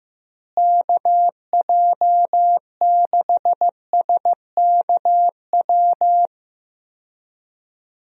Morse code, heard as a keyed tone, about 15 wpm, 700 Hz. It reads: KJ6SKW
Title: USA Call Signs - 6 Characters - Repeated 3x - 15wpm